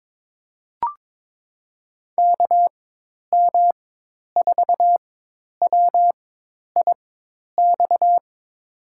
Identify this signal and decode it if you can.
Morse code, heard as a keyed tone, about 22 wpm, spaced as for 10 wpm, 700 Hz. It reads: KM4WIX
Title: USA Call Signs - 6 Characters - 4x Character Spacing - 22wpm